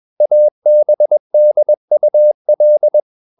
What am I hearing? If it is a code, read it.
Morse code, heard as a keyed tone, about 21 wpm, 600 Hz. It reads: ABDUL
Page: CW med Gnister